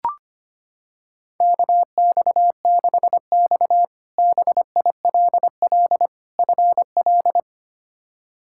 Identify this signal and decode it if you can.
Morse code, heard as a keyed tone, about 25 wpm, 700 Hz. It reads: KX6X BILL FL
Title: K1USN SST QSO Practice - 25wpm